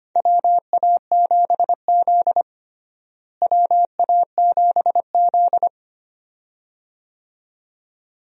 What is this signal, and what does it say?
Morse code, heard as a keyed tone, about 25 wpm, 700 Hz. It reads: WA7Z WA7Z
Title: USA Call Signs - 4 Characters - Repeated 2x - 25wpm